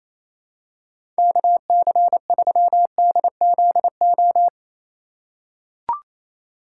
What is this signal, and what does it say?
Morse code, heard as a keyed tone, about 27 wpm, 700 Hz. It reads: KC3DZO
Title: USA Call Signs - 6 Characters - 4x Character Spacing - 28wpm